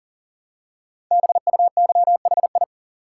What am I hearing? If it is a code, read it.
Morse code, heard as a keyed tone, about 43 wpm, 700 Hz. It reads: BUYHI
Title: Code Groups - 5 Letters - 40wpm